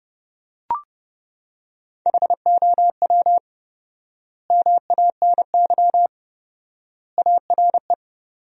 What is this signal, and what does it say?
Morse code, heard as a keyed tone, about 30 wpm, 700 Hz. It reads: HOW MANY ARE
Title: 4x Word-Spacing - Sentences from Top 300 Words - 30wpm